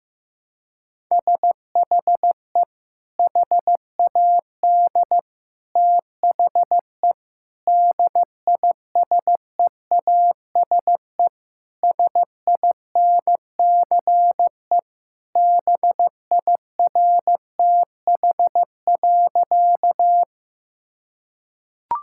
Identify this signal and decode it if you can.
Morse code, heard as a keyed tone, about 15 wpm, 700 Hz. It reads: SHE HAD THE DISEASE SINCE BIRTH.
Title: Sentences from Top 1000 Words - Review Part 6 - 15wpm